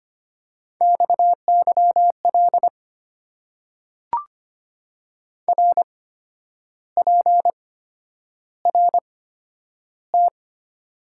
Morse code, heard as a keyed tone, about 25 wpm, spaced as for 10 wpm, 700 Hz. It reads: XYL RPRT
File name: Top 90 QSO Elements - 8x Character Spacing - 25wpm